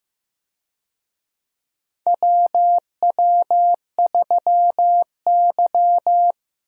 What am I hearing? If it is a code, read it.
Morse code, heard as a keyed tone, about 15 wpm, 700 Hz. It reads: WW3Y
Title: USA Call Signs - 4 Characters - Repeated 3x - 15wpm